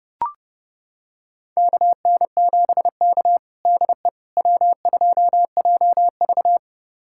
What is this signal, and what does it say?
Morse code, heard as a keyed tone, about 30 wpm, 700 Hz. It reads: KN7K DE W2JV
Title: USA Callsign DE Callsign - 4 Characters - 30wpm